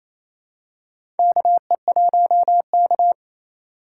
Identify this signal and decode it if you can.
Morse code, heard as a keyed tone, about 28 wpm, 700 Hz. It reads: KE1K
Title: USA Call Signs - 4 Characters - 4x Character Spacing - 28wpm